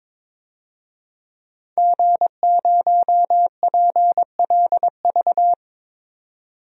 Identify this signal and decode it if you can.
Morse code, heard as a keyed tone, about 22 wpm, 700 Hz. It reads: G0PLV
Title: DXCC-100 Call Signs - 5 Characters - 22wpm